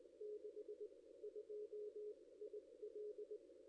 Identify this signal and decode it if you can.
Morse code, heard as a keyed tone, about 19 wpm, spaced as for 16 wpm, 440 Hz. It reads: 62 IL
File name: contests / ss-cw-k5tr-op-n5rz -1600